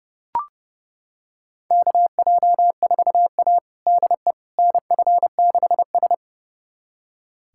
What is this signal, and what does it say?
Morse code, heard as a keyed tone, about 30 wpm, 700 Hz. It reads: KJ4A DE NF6S
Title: USA Callsign DE Callsign - 4 Characters - 30wpm